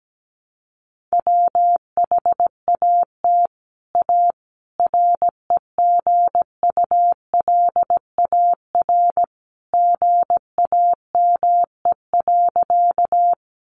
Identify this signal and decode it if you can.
Morse code, heard as a keyed tone, about 17 wpm, 700 Hz. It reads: WHAT A REGULAR GAME.
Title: Sentences from Top 900 Words 17wpm